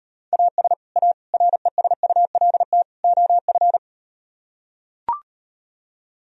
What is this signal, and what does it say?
Morse code, heard as a keyed tone, about 38 wpm, 700 Hz. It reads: AS A RESULT OF